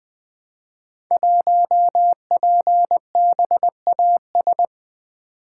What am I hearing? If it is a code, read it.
Morse code, heard as a keyed tone, about 20 wpm, 700 Hz. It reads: 1PBAS
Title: Code Groups - 5 Letter:Numbers - 20wpm